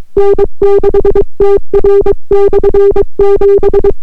Morse code, heard as a keyed tone, about 24 wpm, 400 Hz. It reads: N6TR/7